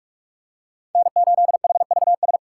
Morse code, heard as a keyed tone, about 46 wpm, 700 Hz. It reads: N8HVS
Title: USA Call Signs - 5 Characters - 4x Character Spacing - 45wpm